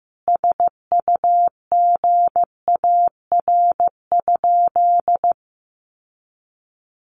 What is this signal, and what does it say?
Morse code, heard as a keyed tone, about 15 wpm, 700 Hz. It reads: SUGAR?
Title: Sentences from Top 1200 Words 15wpm